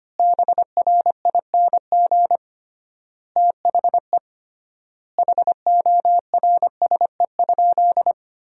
Morse code, heard as a keyed tone, about 25 wpm, 700 Hz. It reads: BRING THE HORSE?
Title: 3x Word-Spacing - Sentences from Top 500 Words - 25wpm